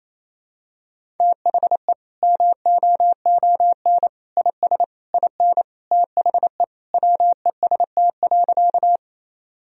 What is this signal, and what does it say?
Morse code, heard as a keyed tone, about 28 wpm, 700 Hz. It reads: THE MOON IS IN THE WEST.
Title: Sentences from Top 1000 Words - Review Part 3 - 28wpm